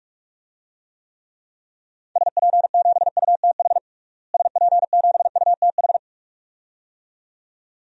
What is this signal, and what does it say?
Morse code, heard as a keyed tone, about 41 wpm, 700 Hz. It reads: SP7UTH SP7UTH
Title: DXCC-100 Call Signs - 6 Characters - Repeated 2x - 45wpm